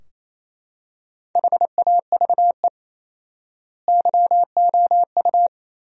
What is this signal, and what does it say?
Morse code, heard as a keyed tone, about 28 wpm, 700 Hz. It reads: HAVE YOU
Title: 4x Word-Spacing - Sentences from Top 300 Words - 28wpm